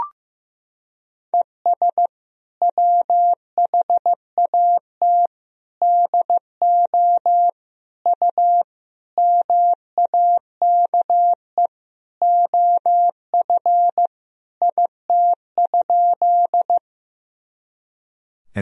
Morse code, heard as a keyed tone, about 16 wpm, 700 Hz. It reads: ES WHAT DO U MAKE OF IT?
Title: Sentences from Top 100 Words and Abbreviations 15wpm